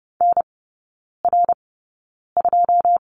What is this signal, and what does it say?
Morse code, heard as a keyed tone, about 30 wpm, 700 Hz. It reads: N R 2
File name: Lesson 19e - Letter w - Three Characters 30wpm